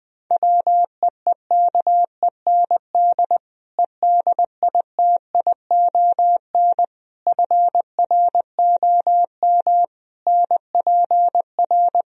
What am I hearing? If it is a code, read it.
Morse code, heard as a keyed tone, about 20 wpm, 700 Hz. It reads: WEEKEND EDITION FROM NPR
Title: Speed-Racing - Sets of 5 Words 17wpm